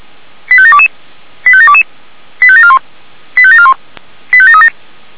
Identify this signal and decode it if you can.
Five-tone selective call (ZVEI), heard as quick runs of five tones, tones of about 70 ms. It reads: 85722 85722 85741 85741 85737